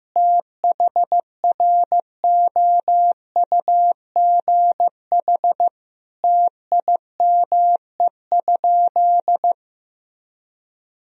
Morse code, heard as a keyed tone, about 16 wpm, 700 Hz. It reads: THROUGH TIME?
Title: Sentences from Top 1000 Words - Review Part 2 - 15wpm